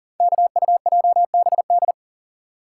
Morse code, heard as a keyed tone, about 39 wpm, 700 Hz. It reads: KUJBD
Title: Code Groups - 5 Letter:Numbers - 40wpm